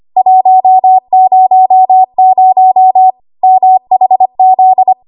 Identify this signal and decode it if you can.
Morse code, heard as a keyed tone, about 25 wpm, 750 Hz. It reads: 100 MHZ
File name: Archive / 25 WPM / 25WPM